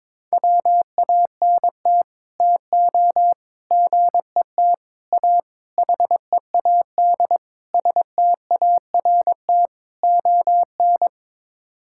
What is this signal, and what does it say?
Morse code, heard as a keyed tone, about 22 wpm, 700 Hz. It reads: WANT TO GET A HEAD START ON
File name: practice / 22wpm / Sets of 7 Words - 22wpm